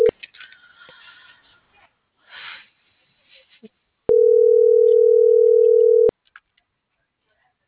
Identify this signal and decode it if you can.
Ringback tone (North America), 440 + 480 Hz.
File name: CENTRAL GOMSA / out lizeth.morales